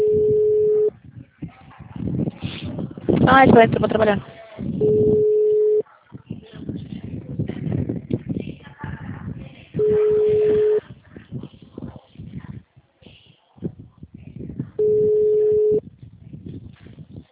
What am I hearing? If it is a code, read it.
Ringback tone (Europe, 425 Hz), with speech.